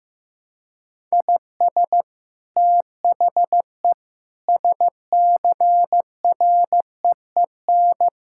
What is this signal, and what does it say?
Morse code, heard as a keyed tone, about 16 wpm, 700 Hz. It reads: IS THE SCREEN
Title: Sentences from Top 1000 Words - Review Part 6 - 15wpm